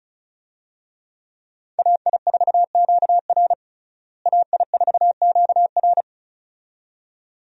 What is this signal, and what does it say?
Morse code, heard as a keyed tone, about 35 wpm, 700 Hz. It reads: AI4QR AI4QR
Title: USA Call Signs - 5 Characters - Repeated 2x - 35wpm